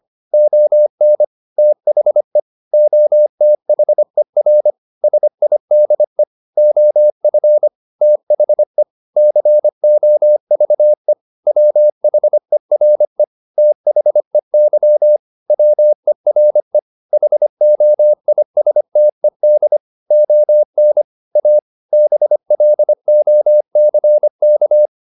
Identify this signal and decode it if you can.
Morse code, heard as a keyed tone, about 25 wpm, 600 Hz. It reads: ON THE OTHER SIDE OF THE COVE WHERE THEY WERE HOISTED ON A BLOCK